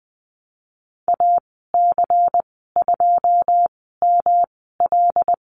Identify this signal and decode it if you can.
Morse code, heard as a keyed tone, about 20 wpm, spaced as for 15 wpm, 700 Hz. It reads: AC2ML
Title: 122 Lesson 20g - Number 3 - Call Signs 20wpm